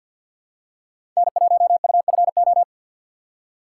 Morse code, heard as a keyed tone, about 51 wpm, 700 Hz. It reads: N1UFO
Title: USA Call Signs - 5 Characters - 6x Character Spacing - 50wpm